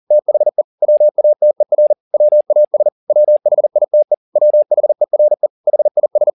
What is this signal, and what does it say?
Morse code, heard as a keyed tone, about 40 wpm, 600 Hz. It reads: THE WATER WAS WHITE WHERE HIS
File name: the old man and the sea 2 40wpm